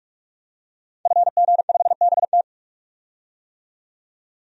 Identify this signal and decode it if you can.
Morse code, heard as a keyed tone, about 43 wpm, 700 Hz. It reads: FG5BT
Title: DXCC-100 Call Signs - 5 Characters - 45wpm